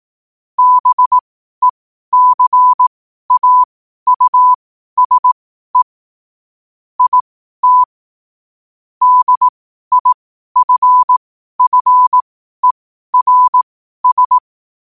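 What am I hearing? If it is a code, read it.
Morse code, heard as a keyed tone, about 18 wpm, spaced as for 12 wpm, 1000 Hz. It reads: BECAUSE IT DIFFERS
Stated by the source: Morse practice sent at 12 wpm.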